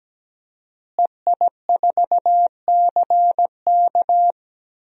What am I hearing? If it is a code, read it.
Morse code, heard as a keyed tone, about 17 wpm, 700 Hz. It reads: EI4CK
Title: DXCC-100 Call Signs - 5 Characters - Repeated 3x - 17wpm